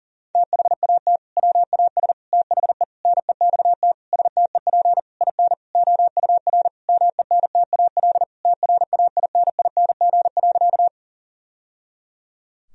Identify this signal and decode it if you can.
Morse code, heard as a keyed tone, about 37 wpm, 700 Hz. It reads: THAT WAS THE NEXT STEP IN OUR MENTAL TRAINING.
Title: Sentences from Top 1000 Words - Review Part 4 - 40wpm